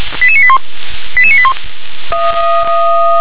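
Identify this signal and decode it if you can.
Five-tone selective call (ZVEI), heard as quick runs of five tones, tones of about 70 ms. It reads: 88981 88981
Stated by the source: Five-tone ZVEI selective call received by SDR.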